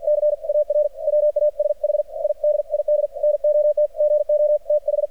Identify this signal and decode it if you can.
Morse code, heard as a keyed tone, about 53 wpm, 600 Hz, with radio noise. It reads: CAR WASH I DID NOT GO TH
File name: archived / CRSnet